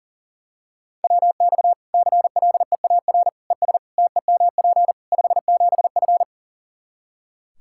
Morse code, heard as a keyed tone, about 40 wpm, 700 Hz. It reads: WX CLEAR ES TEMP 57F
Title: Practice QSO - Exchange 2 - Info - 40wpm